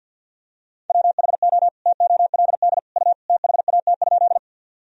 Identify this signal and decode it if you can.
Morse code, heard as a keyed tone, about 52 wpm, 700 Hz. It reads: WHO TOLD U THAT?
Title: Sentences from Top 400 Words and Abbreviations 50wpm